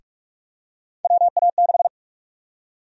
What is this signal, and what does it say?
Morse code, heard as a keyed tone, about 46 wpm, 700 Hz. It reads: WA6
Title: USA Call Sign Prefixes - 45wpm